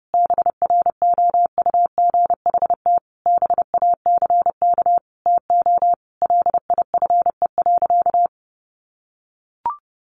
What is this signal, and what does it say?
Morse code, heard as a keyed tone, about 30 wpm, 700 Hz. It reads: BROUGHT BACK TO LIFE.
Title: Sentences from Top 600 Words 30wpm